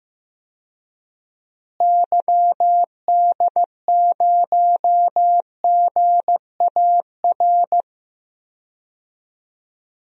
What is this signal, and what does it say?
Morse code, heard as a keyed tone, about 16 wpm, 700 Hz. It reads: YD0GAR
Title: DXCC-100 Call Signs - 6 Characters - Repeated 2x - 15wpm